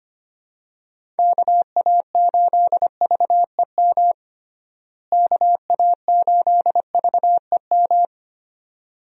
Morse code, heard as a keyed tone, about 25 wpm, 700 Hz. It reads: KA8VEM KA8VEM
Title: USA Call Signs - 6 Characters - Repeated 2x - 25wpm